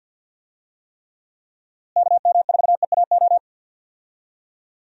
Morse code, heard as a keyed tone, about 51 wpm, 700 Hz. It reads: KM4EAO
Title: USA Call Signs - 6 Characters - Repeated 3x - 50wpm